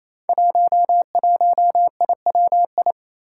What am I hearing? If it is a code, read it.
Morse code, heard as a keyed tone, about 27 wpm, 700 Hz. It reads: 11IWI
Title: Five Challenging Letter-Number Combinations 28wpm